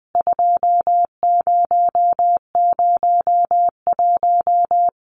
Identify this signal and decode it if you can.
Morse code, heard as a keyed tone, about 20 wpm, 700 Hz. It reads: 2001
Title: CWT Exchange 20wpm